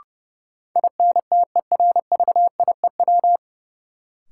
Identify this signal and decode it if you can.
Morse code, heard as a keyed tone, about 31 wpm, 700 Hz. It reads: INTERVIEW
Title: Sentences from Top 1600 Words - New Word Drill - 30wpm